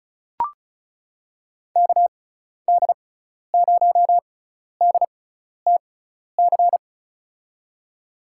Morse code, heard as a keyed tone, about 36 wpm, spaced as for 12 wpm, 700 Hz. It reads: KD0DTC
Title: USA Call Signs - 6 Characters - 6x Character Spacing - 35wpm